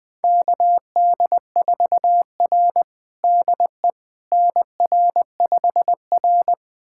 Morse code, heard as a keyed tone, about 20 wpm, 700 Hz. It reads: KD4R DE NR5R
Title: USA Callsign DE Callsign - 4 Characters - 20wpm